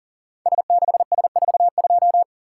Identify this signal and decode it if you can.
Morse code, heard as a keyed tone, about 40 wpm, 700 Hz. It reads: S6S42
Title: Code Groups - 5 Letter:Numbers - 40wpm